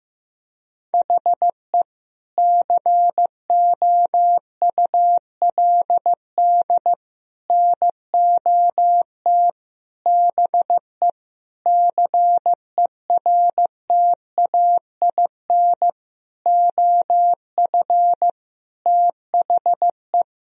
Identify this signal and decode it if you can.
Morse code, heard as a keyed tone, about 15 wpm, 700 Hz. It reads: HE COULD NOT BE CERTAIN OF THE
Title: Sentences from Top 1000 Words - Review Part 2 - 15wpm